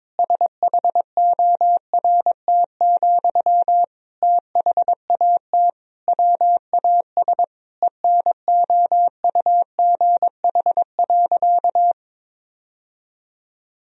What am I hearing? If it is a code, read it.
Morse code, heard as a keyed tone, about 22 wpm, 700 Hz. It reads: SHORT, THAT WAS ENOUGH.
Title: Sentences from Top 1000 Words - Review Part 2 - 22wpm